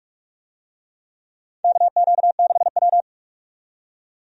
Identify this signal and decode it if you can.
Morse code, heard as a keyed tone, about 46 wpm, 700 Hz. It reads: KQ6W
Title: USA Call Signs - 4 Characters - 45wpm